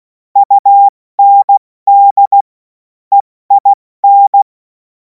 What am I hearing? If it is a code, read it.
Morse code, heard as a keyed tone, about 16 wpm, 800 Hz. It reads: UND EIN
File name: Morsekurs / Witze / Witz 24. Februar 2026 70 F80